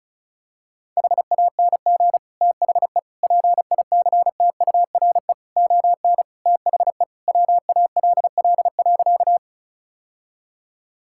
Morse code, heard as a keyed tone, about 36 wpm, 700 Hz. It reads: HANG THE PICTURE ON THE WALL.
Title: Sentences from Top 1600 Words 35wpm